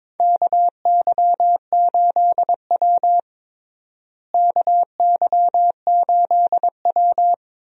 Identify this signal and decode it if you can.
Morse code, heard as a keyed tone, about 21 wpm, 700 Hz. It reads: KY8W KY8W
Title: USA Call Signs - 4 Characters - Repeated 3x - 22wpm